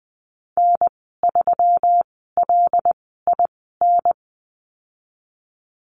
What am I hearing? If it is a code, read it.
Morse code, heard as a keyed tone, about 20 wpm, spaced as for 15 wpm, 700 Hz. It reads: N3LIN
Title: 128 Lesson 21g - Number 6 - Call Signs 20wpm